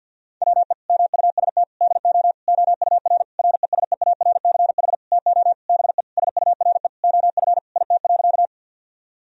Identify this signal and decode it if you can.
Morse code, heard as a keyed tone, about 50 wpm, 700 Hz. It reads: WE MUST DO OUR RESEARCH TO BE SURE OF IT.
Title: Sentences from Top 1000 Words - Review Part 4 - 50wpm